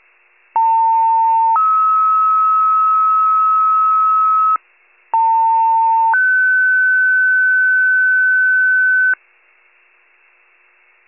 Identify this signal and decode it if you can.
Two-tone page, heard as a one-second tone, then a three-second tone, about 900 then 1290 Hz, about 900 then 1500 Hz. It is heard from 0.6 to 9.1 s.